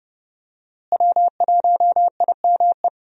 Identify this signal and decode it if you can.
Morse code, heard as a keyed tone, about 30 wpm, 700 Hz. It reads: W1IME
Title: USA Call Signs - 5 Characters - Repeated 2x - 30wpm